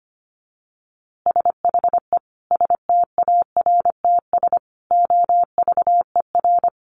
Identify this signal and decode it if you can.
Morse code, heard as a keyed tone, about 25 wpm, 700 Hz. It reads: SHE STARTS OVER
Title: Easy Phrases 25wpm